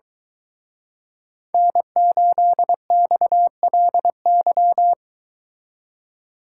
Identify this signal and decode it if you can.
Morse code, heard as a keyed tone, about 23 wpm, 700 Hz. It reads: N8XLY